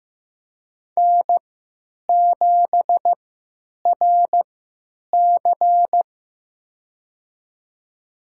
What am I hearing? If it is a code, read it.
Morse code, heard as a keyed tone, about 15 wpm, spaced as for 9 wpm, 700 Hz. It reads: N7RC
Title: USA Call Signs - 4 Characters - 3x Character Spacing - 15wpm